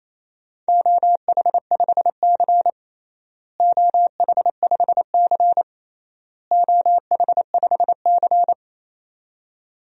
Morse code, heard as a keyed tone, about 28 wpm, 700 Hz. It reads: OH5C OH5C OH5C
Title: DXCC-100 Call Signs - 4 Characters - Repeated 3x - 28wpm